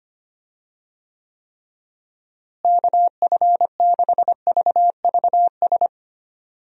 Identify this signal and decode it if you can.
Morse code, heard as a keyed tone, about 25 wpm, 700 Hz. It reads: KF6VVS
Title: USA Call Signs - 6 Characters - Repeated 3x - 25wpm